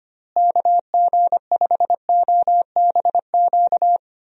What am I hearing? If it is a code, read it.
Morse code, heard as a keyed tone, about 25 wpm, 700 Hz. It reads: KG5OBQ
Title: USA Call Signs - 6 Characters - Repeated 2x - 25wpm